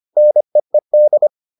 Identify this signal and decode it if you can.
Morse code, heard as a keyed tone, about 25 wpm, 600 Hz. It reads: NEED